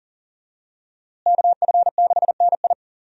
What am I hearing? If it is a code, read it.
Morse code, heard as a keyed tone, about 39 wpm, 700 Hz. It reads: KF6NI
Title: USA Call Signs - 5 Characters - Repeated 3x - 40wpm